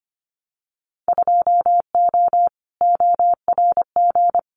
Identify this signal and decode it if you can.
Morse code, heard as a keyed tone, about 25 wpm, 700 Hz. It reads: 2O ORG
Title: Winter Field Day Exchanges 25wpm